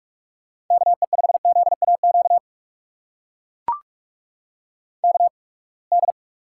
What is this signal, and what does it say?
Morse code, heard as a keyed tone, about 42 wpm, spaced as for 30 wpm, 700 Hz. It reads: KE5ZAQ KD
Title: USA Call Signs - 6 Characters - 8x Character Spacing - 45wpm